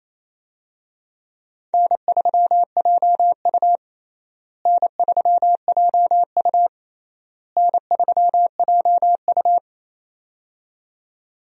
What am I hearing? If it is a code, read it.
Morse code, heard as a keyed tone, about 28 wpm, 700 Hz. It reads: N3JU N3JU N3JU
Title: USA Call Signs - 4 Characters - Repeated 3x - 28wpm